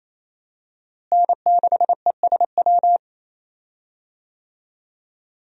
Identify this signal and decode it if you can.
Morse code, heard as a keyed tone, about 28 wpm, 700 Hz. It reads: N6ESW
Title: USA Call Signs - 5 Characters - Repeated 2x - 28wpm